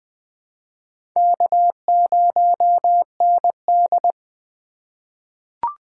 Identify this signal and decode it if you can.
Morse code, heard as a keyed tone, about 20 wpm, 700 Hz. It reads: K0ND
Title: USA Call Signs - 4 Characters - 4x Character Spacing - 20wpm